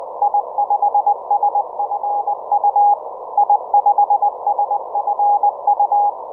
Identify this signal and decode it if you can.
Morse code, heard as a keyed tone, about 20 wpm, 820 Hz. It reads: I5SFU I5SFU